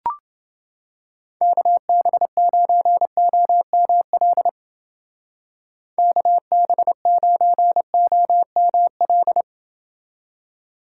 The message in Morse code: KB9OML KB9OML

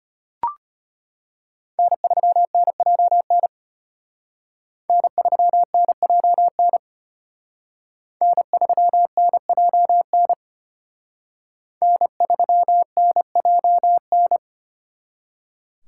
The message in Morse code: N3NJN N3NJN N3NJN N3NJN

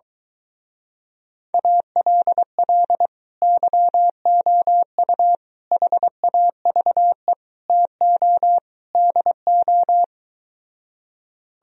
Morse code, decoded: ALL YOU HAVE TO DO

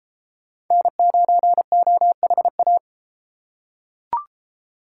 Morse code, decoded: N9OHA

